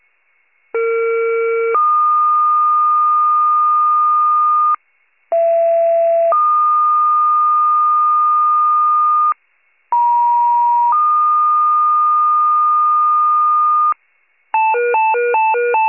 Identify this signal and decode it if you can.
Two-tone page, heard as a one-second tone, then a three-second tone, about 460 then 1220 Hz, about 670 then 1220 Hz, about 950 then 1220 Hz. It is heard from 0.7 to 13.9 s.